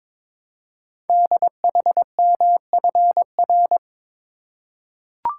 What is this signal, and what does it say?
Morse code, heard as a keyed tone, about 23 wpm, 700 Hz. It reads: DHMFR
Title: Code Groups - 5 Letters - 22wpm